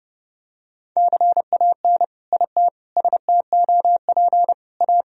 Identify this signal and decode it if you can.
Morse code, heard as a keyed tone, about 28 wpm, 700 Hz. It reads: CAN IT STOP A
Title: Sentences from Top 1000 Words - Review Part 2 - 30wpm